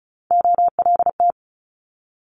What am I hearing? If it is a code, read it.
Morse code, heard as a keyed tone, about 36 wpm, 700 Hz. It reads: OLT
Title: Multi-letter Phonemes Quick Fire 35wpm